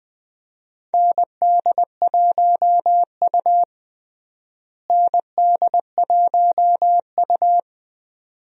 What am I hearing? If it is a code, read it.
Morse code, heard as a keyed tone, about 20 wpm, 700 Hz. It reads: ND1U ND1U
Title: USA Call Signs - 4 Characters - Repeated 2x - 20wpm